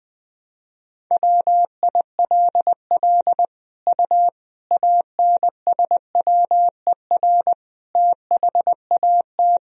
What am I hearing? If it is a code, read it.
Morse code, heard as a keyed tone, about 20 wpm, 700 Hz. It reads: WILL U ANSWER THAT
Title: Sentences from Top 400 Words and Abbreviations 20wpm